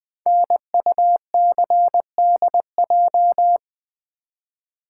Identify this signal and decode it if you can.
Morse code, heard as a keyed tone, about 20 wpm, 700 Hz. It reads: NUCDJ
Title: Code Groups - 5 Letters - 20wpm